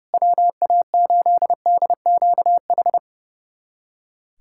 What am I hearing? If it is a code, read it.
Morse code, heard as a keyed tone, about 32 wpm, 700 Hz. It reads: WA8DQH
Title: USA Callsign DE Callsign 30wpm